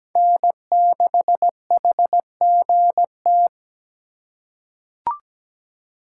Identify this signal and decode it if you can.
Morse code, heard as a keyed tone, about 17 wpm, 700 Hz. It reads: N6HGT
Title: USA Call Signs - 4x Character Spacing - 17wpm